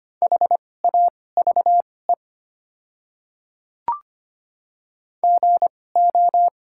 Morse code, heard as a keyed tone, about 25 wpm, spaced as for 18 wpm, 700 Hz. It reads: HAVE GO
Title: 2x Character Spacing - Top 100 Words - 25wpm